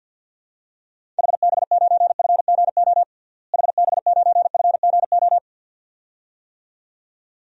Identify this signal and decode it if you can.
Morse code, heard as a keyed tone, about 46 wpm, 700 Hz. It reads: HB9FGO HB9FGO
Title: DXCC-100 Call Signs - 6 Characters - Repeated 2x - 50wpm